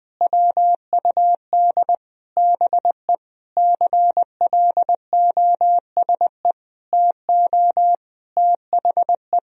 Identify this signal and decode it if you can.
Morse code, heard as a keyed tone, about 20 wpm, 700 Hz. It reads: WUD BE CLOSE TO THE
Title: Sentences from Top 300 Words and Abbreviations 20wpm